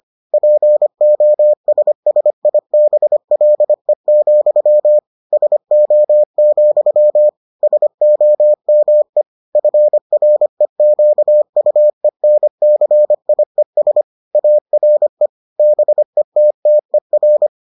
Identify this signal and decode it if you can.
Morse code, heard as a keyed tone, about 25 wpm, 600 Hz. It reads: POSSIBLE, SO, SOME FREQUENCIES ARE BETTER